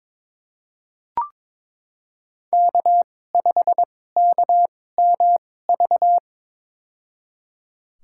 Morse code, heard as a keyed tone, about 22 wpm, spaced as for 16 wpm, 700 Hz. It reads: K5KMV
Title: USA Call Signs - 5 Characters - 2x Character Spacing - 22wpm